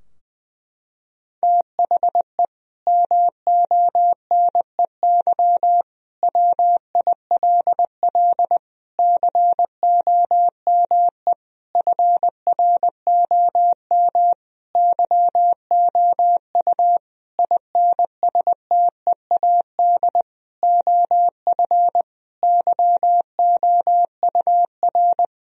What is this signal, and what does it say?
Morse code, heard as a keyed tone, about 20 wpm, 700 Hz. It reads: THE MONEY WILL COME FROM YOU INSTEAD OF YOUR
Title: Sentences from Top 1000 Words - Review Part 4 - 20wpm